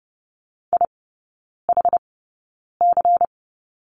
Morse code, heard as a keyed tone, about 30 wpm, spaced as for 9 wpm, 700 Hz. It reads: IHC
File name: Lesson 16e - Letter c - Three Characters 30wpm